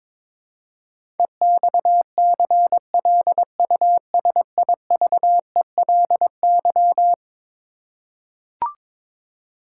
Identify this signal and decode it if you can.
Morse code, heard as a keyed tone, about 22 wpm, 700 Hz. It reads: EXCLUSIVELY
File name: practice / 22wpm / Eleven Letter Words - 22wpm